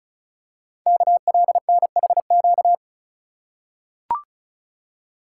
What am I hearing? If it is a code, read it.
Morse code, heard as a keyed tone, about 36 wpm, 690 Hz. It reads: KLNHQ
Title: Code Groups - 5 Letters - 35wpm